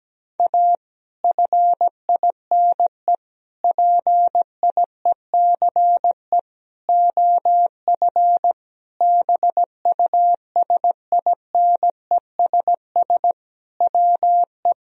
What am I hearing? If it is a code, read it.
Morse code, heard as a keyed tone, about 17 wpm, 700 Hz. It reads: A FINE PIECE OF BUSINESS WE
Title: Sentences from Top 1000 Words - Review Part 2 - 17wpm